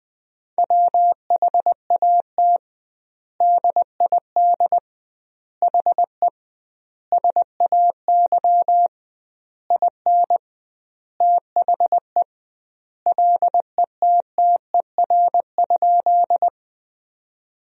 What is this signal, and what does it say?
Morse code, heard as a keyed tone, about 20 wpm, 700 Hz. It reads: WHAT DID HE SAY IN THE LETTER?
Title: 2x Word-Spacing - Sentences from Top 300 Words - 20wpm